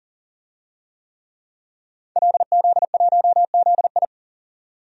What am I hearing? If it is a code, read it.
Morse code, heard as a keyed tone, about 40 wpm, 700 Hz. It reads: LZ1ZI
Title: DXCC-100 Call Signs - 5 Characters - 40wpm